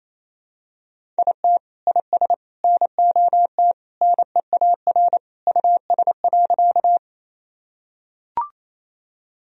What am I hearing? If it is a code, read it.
Morse code, heard as a keyed tone, about 28 wpm, 700 Hz. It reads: IT IS NOT NEAR US.
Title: Sentences from Top 1000 Words - Review Part 1 - 28wpm